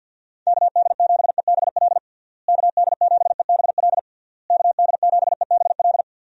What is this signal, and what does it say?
Morse code, heard as a keyed tone, about 52 wpm, 700 Hz. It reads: KD7EBL KD7EBL KD7EBL
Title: USA Call Signs - 6 Characters - Repeated 3x - 50wpm